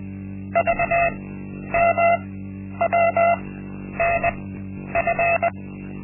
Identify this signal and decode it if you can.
Morse code, heard as a keyed tone, about 20 wpm, spaced as for 10 wpm, 680 Hz. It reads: VMWNF
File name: Linksmakalnis-1